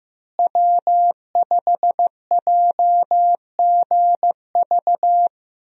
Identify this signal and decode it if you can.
Morse code, heard as a keyed tone, about 16 wpm, 700 Hz. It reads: W5JGV
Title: USA Call Signs - 5 Characters - Repeated 2x - 15wpm